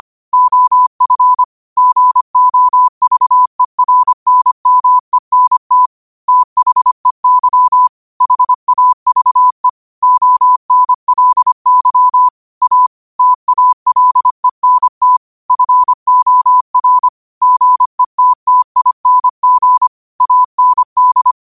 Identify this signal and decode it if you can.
Morse code, heard as a keyed tone, about 25 wpm, 1000 Hz. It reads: OF GOVERNMENT THEY HAVE ONLY A TALENT FOR GETTING AND
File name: qotdfiles / 25wpm